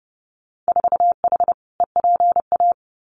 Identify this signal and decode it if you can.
Morse code, heard as a keyed tone, about 32 wpm, 700 Hz. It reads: 4H EPA
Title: Winter Field Day Exchanges 30wpm